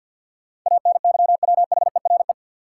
Morse code, heard as a keyed tone, about 50 wpm, 700 Hz. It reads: ANYWHERE
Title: Sentences from Top 1400 Words - New Word Drill - 50wpm